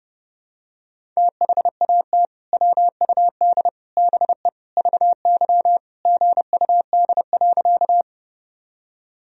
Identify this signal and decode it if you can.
Morse code, heard as a keyed tone, about 32 wpm, 700 Hz. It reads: THAT WUD BE VY GUD.
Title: Sentences from Top 200 Words and Abbreviations 30wpm